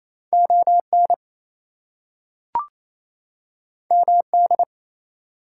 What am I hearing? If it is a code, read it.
Morse code, heard as a keyed tone, about 28 wpm, 700 Hz. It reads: ON MD
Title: US States and Canadian Provinces 28wpm